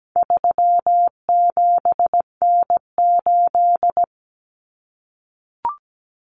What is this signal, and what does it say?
Morse code, heard as a keyed tone, about 17 wpm, 700 Hz. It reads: 37N8
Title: Common Cut Numbers - 4-Digit 17wpm